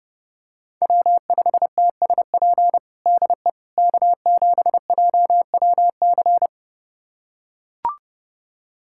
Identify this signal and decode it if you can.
Morse code, heard as a keyed tone, about 30 wpm, 700 Hz. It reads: W5TSP DE K7JWC
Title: USA Callsign DE Callsign - 5 Characters - 30wpm